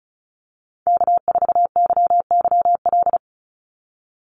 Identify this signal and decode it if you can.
Morse code, heard as a keyed tone, about 34 wpm, 700 Hz. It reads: K4YYL
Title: Call Signs Repeated 2x - 35wpm